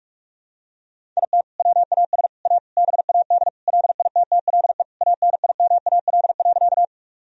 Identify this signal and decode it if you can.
Morse code, heard as a keyed tone, about 48 wpm, 700 Hz. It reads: IT WAS A BAD LITTLE ANIMAL.
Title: Sentences from Top 1000 Words - Review Part 1 - 45wpm